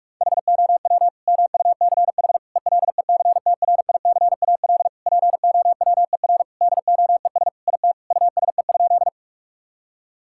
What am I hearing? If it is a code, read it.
Morse code, heard as a keyed tone, about 44 wpm, 700 Hz. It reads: HOW MUCH ELECTRICAL POWER DOES IT USE?
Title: Sentences from Top 1000 Words - Review Part 6 - 45wpm